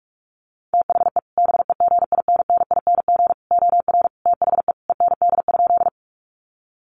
Morse code, heard as a keyed tone, about 44 wpm, 700 Hz. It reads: THE BEGINNING OR THE END?
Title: Sentences from Top 800 Words 45wpm